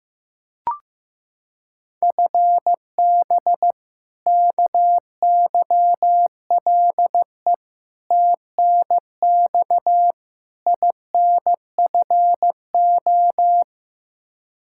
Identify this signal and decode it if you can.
Morse code, heard as a keyed tone, about 15 wpm, 700 Hz. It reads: FB KYLE TNX INFO
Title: Practice QSO - Exchange 3 - More Info - 15wpm